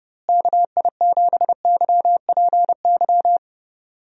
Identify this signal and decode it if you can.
Morse code, heard as a keyed tone, about 30 wpm, 700 Hz. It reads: KI7YPY
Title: USA Call Signs - 6 Characters - Repeated 2x - 30wpm